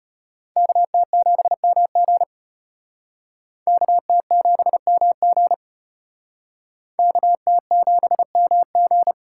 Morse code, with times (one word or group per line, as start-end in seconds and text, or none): 0.6-2.2 KT7MG
3.7-5.5 KT7MG
7.0-9.1 KT7MG